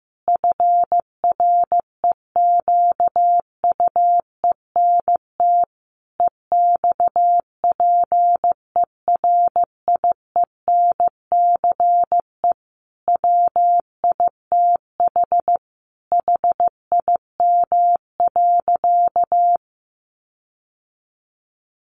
Morse code, heard as a keyed tone, about 15 wpm, 700 Hz. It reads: FREQUENT EXPERIENCE WITH HIM.
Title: Sentences from Top 1400 Words 15wpm